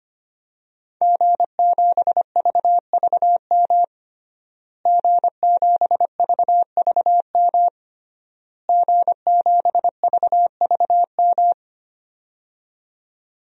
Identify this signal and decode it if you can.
Morse code, heard as a keyed tone, about 25 wpm, 700 Hz. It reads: G7VVM G7VVM G7VVM
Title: DXCC-100 Call Signs - 5 Characters - Repeated 3x - 25wpm